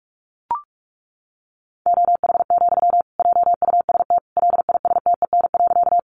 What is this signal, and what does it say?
Morse code, heard as a keyed tone, about 44 wpm, 700 Hz. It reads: OH, JUST LISTEN.
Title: Sentences from Top 500 Words 45wpm